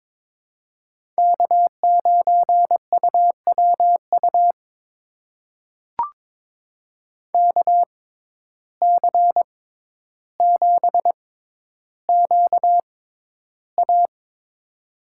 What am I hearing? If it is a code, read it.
Morse code, heard as a keyed tone, about 22 wpm, spaced as for 11 wpm, 700 Hz. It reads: K9UWU KC7QA
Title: USA Call Signs - 5 Characters - 6x Character Spacing - 22wpm